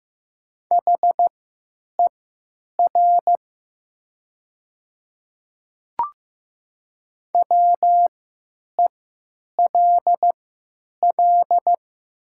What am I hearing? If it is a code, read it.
Morse code, heard as a keyed tone, about 15 wpm, spaced as for 9 wpm, 700 Hz. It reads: HER WELL
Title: 3x Character Spacing - Top 200 Words - 15wpm